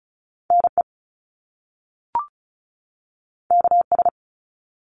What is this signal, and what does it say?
Morse code, heard as a keyed tone, about 36 wpm, 700 Hz. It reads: NE KS
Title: US States 35wpm